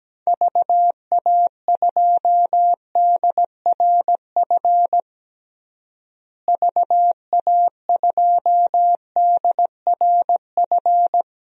Morse code, heard as a keyed tone, about 17 wpm, 700 Hz. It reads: VA2DRF VA2DRF
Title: DXCC-100 Call Signs - 6 Characters - Repeated 3x - 17wpm